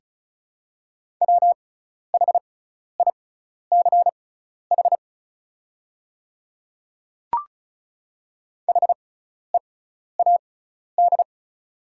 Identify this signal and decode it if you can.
Morse code, heard as a keyed tone, about 36 wpm, spaced as for 12 wpm, 690 Hz. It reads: WHICH HEAD
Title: 6x Character Spacing - Top 500 Words - 35wpm